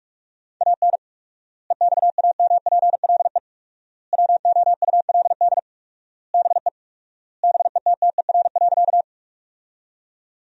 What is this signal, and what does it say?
Morse code, heard as a keyed tone, about 46 wpm, 700 Hz. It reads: AN EXAMPLE WOULD BE BETTER.
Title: 4x Word-Spacing - Sentences from Top 400 Words - 45wpm